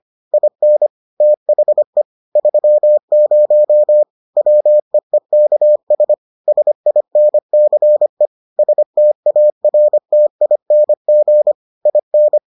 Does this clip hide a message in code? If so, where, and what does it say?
Yes: Morse code from 0.3 to 12.4 s, heard as a keyed tone, about 25 wpm, 600 Hz. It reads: IN THE 30 WEEKS SINCE STARTING IN